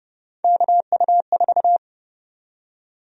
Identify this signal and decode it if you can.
Morse code, heard as a keyed tone, about 30 wpm, 700 Hz. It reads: KU4
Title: USA Call Sign Prefixes - 30wpm